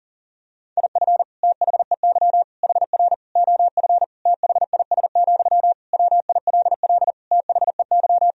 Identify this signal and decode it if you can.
Morse code, heard as a keyed tone, about 40 wpm, 700 Hz. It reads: IF THEY HR OF THIS, WILL THEY
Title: Sentences from Top 400 Words and Abbreviations 40wpm